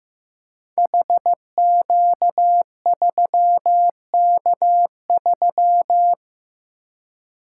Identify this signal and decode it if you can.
Morse code, heard as a keyed tone, about 15 wpm, 700 Hz. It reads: HQ3K3
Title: Code Groups - 5 Letter:Numbers - 15wpm